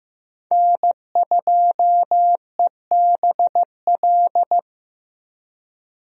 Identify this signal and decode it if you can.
Morse code, heard as a keyed tone, about 15 wpm, 700 Hz. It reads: N2EBL
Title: USA Call Signs - 5 Characters - Repeated 2x - 15wpm